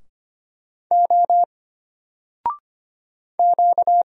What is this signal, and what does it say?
Morse code, heard as a keyed tone, about 25 wpm, 700 Hz. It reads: O Q
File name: Single-Letter-Number-25wpm